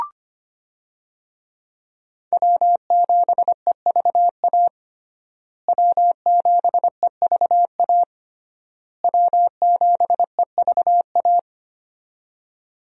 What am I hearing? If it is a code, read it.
Morse code, heard as a keyed tone, about 25 wpm, 700 Hz. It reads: W7EVA W7EVA W7EVA
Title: USA Call Signs - 5 Characters - Repeated 3x - 25wpm